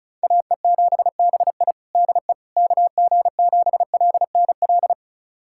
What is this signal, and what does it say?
Morse code, heard as a keyed tone, about 34 wpm, 700 Hz. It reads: AE7BI DE KG7LNL
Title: USA Callsign DE Callsign 35wpm